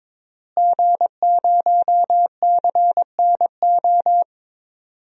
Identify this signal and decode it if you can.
Morse code, heard as a keyed tone, about 22 wpm, 700 Hz. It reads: G0CNO
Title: DXCC-100 Call Signs - 5 Characters - Repeated 3x - 22wpm